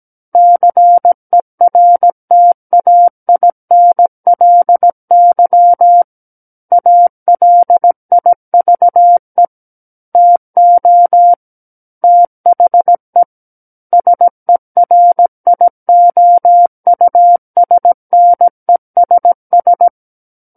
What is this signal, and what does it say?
Morse code, heard as a keyed tone, about 17 wpm, 700 Hz. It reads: CERTAINLY ALIVE TO THE SERIOUSNESS
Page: SKCC Learning Center - Morse Code Practice Files